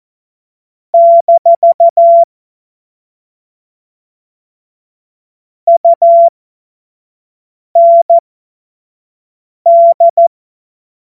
Morse code, heard as a keyed tone, about 14 wpm, spaced as for 5 wpm, 680 Hz. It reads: - UND
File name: Morsekurs / Witze / Witz 21. Dezember 2021, 50 F70